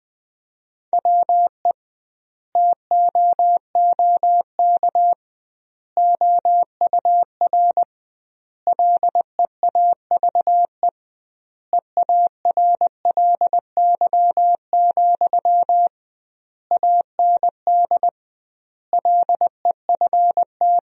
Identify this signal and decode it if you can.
Morse code, heard as a keyed tone, about 21 wpm, 700 Hz. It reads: WE TOOK OUR LEAVE EARLY, AND LEFT
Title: 2x Word-Spacing - Sentences from Top 400 Words - 20wpm